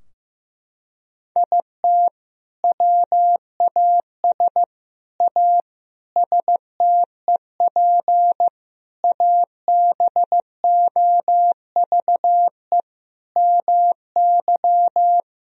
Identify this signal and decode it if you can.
Morse code, heard as a keyed tone, about 15 wpm, 700 Hz. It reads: IT WAS A STEP ABOVE MY
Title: Sentences from Top 1000 Words - Review Part 3 - 15wpm